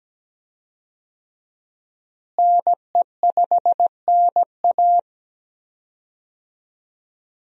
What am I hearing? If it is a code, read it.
Morse code, heard as a keyed tone, about 17 wpm, 700 Hz. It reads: NE5NA
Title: USA Call Signs - 5 Characters - Repeated 2x - 17wpm